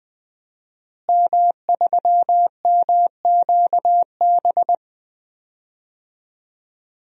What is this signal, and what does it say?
Morse code, heard as a keyed tone, about 20 wpm, 700 Hz. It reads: M3MQB
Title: DXCC-100 Call Signs - 5 Characters - Repeated 3x - 20wpm